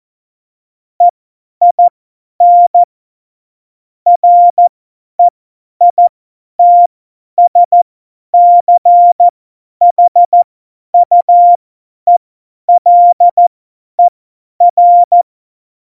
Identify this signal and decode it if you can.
Morse code, heard as a keyed tone, about 14 wpm, spaced as for 10 wpm, 700 Hz. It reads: EIN REITSCHUELER
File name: Morsekurs / Witze / Witz 04 August 2020, 50 F70